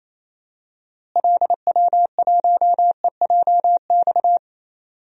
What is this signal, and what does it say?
Morse code, heard as a keyed tone, about 28 wpm, 700 Hz. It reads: LW1EJX
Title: DXCC-100 Call Signs - 6 Characters - Repeated 2x - 28wpm